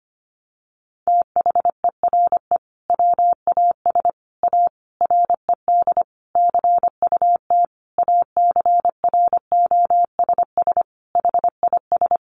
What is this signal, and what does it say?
Morse code, heard as a keyed tone, about 25 wpm, 700 Hz. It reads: THERE WAS A RED CUT ACROSS HIS
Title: Sentences from Top 700 Words 25wpm